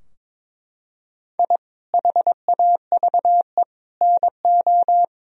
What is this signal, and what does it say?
Morse code, heard as a keyed tone, about 22 wpm, 700 Hz. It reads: I HAVE NO
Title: Sentences from Top 1000 Words - Review Part 3 - 22wpm